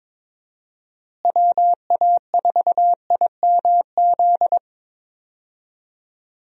Morse code, heard as a keyed tone, about 22 wpm, 700 Hz. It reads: WA4IMZ
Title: USA Call Signs - 6 Characters - Repeated 3x - 22wpm